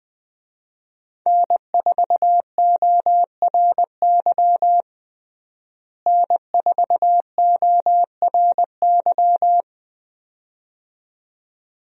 Morse code, heard as a keyed tone, about 21 wpm, 700 Hz. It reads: N4ORY N4ORY
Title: USA Call Signs - 5 Characters - Repeated 2x - 20wpm